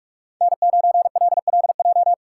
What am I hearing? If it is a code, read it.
Morse code, heard as a keyed tone, about 44 wpm, 700 Hz. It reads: N9LLJ